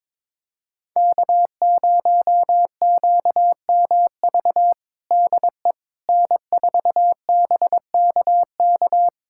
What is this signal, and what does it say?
Morse code, heard as a keyed tone, about 22 wpm, 700 Hz. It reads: K0QMV DE N4BKK
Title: USA Callsign DE Callsign - 5 Characters - 22wpm